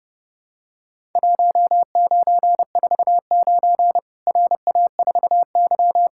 Morse code, heard as a keyed tone, about 30 wpm, 700 Hz. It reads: 1949 RA4Y
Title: Worked All Europe DX Contest - QTC Exchange - 30wpm